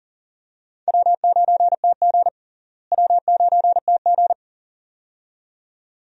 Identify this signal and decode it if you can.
Morse code, heard as a keyed tone, about 40 wpm, 700 Hz. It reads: W9TG W9TG
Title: USA Call Signs - 4 Characters - Repeated 2x - 40wpm